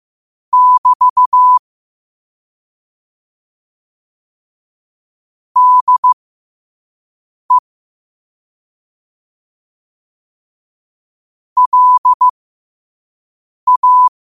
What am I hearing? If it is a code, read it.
Morse code, heard as a keyed tone, about 15 wpm, spaced as for 5 wpm, 1000 Hz. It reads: <BT> DE LA
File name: qotdfiles / 5wpm